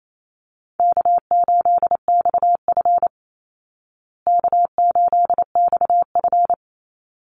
Morse code, heard as a keyed tone, about 28 wpm, 700 Hz. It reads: K8XF K8XF
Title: Call Signs Repeated 3x - 28wpm